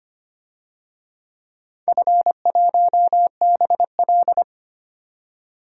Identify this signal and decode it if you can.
Morse code, heard as a keyed tone, about 25 wpm, 700 Hz. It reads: F1BL
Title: DXCC-100 Call Signs - 4 Characters - 25wpm